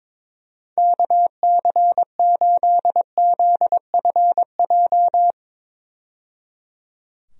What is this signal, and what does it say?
Morse code, heard as a keyed tone, about 22 wpm, 700 Hz. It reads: KC8ZFJ
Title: USA Call Signs - 6 Characters - 22wpm